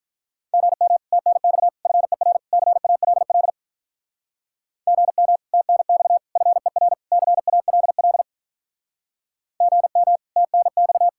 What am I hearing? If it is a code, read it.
Morse code, heard as a keyed tone, about 43 wpm, 700 Hz. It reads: GM TNX FER CALL GM TNX FER CALL GM TNX
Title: Speed-Racing Common QSO Phrases - 35wpm